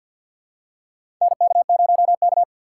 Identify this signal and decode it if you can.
Morse code, heard as a keyed tone, about 47 wpm, 700 Hz. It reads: NK0K
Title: USA Call Signs - 6x Character Spacing - 50wpm